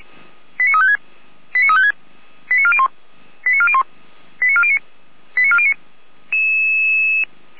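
Five-tone selective call (ZVEI), heard as quick runs of five tones, tones of about 70 ms. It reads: 89356 89356 89401 89401 89409 89409
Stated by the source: Five-tone ZVEI selective call received by SDR, from the Feuerwehr Orlamünde (fire brigade) audio server.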